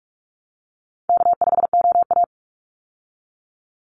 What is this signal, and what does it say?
Morse code, heard as a keyed tone, about 44 wpm, 700 Hz. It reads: K5OA
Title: Call Signs Repeated 2x - 45wpm